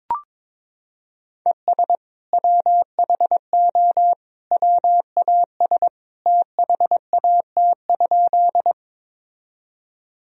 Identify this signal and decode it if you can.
Morse code, heard as a keyed tone, about 22 wpm, 700 Hz. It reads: ES WHO WAS THAT?
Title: Sentences from Top 100 Words and Abbreviations 22wpm